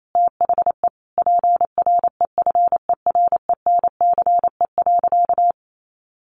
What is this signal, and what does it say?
Morse code, heard as a keyed tone, about 28 wpm, 700 Hz. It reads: THE PREFERENCE.
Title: Sentences from Top 1500 Words 28wpm